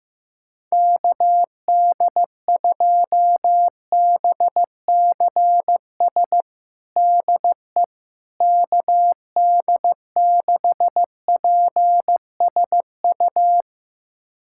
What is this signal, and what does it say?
Morse code, heard as a keyed tone, about 16 wpm, 700 Hz. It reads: KD2BCS DE KD6PSU
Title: USA Callsign DE Callsign 15wpm